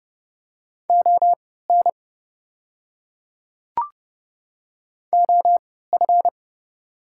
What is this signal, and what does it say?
Morse code, heard as a keyed tone, about 30 wpm, spaced as for 17 wpm, 690 Hz. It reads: ON OF
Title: 3x Character Spacing - Top 200 Words - 30wpm